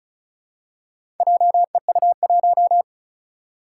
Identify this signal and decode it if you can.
Morse code, heard as a keyed tone, about 36 wpm, 700 Hz. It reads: JEU1
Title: Four Challenging Letter-Number Combinations 35wpm